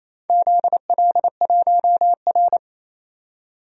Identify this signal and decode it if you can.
Morse code, heard as a keyed tone, about 28 wpm, 700 Hz. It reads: ZL1R
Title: DXCC-100 Call Signs - 4 Characters - 28wpm